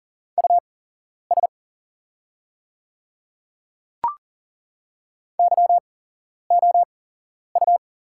Morse code, heard as a keyed tone, about 40 wpm, spaced as for 11 wpm, 700 Hz. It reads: US YOU
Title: 8x Character Spacing - Top 200 Words - 40wpm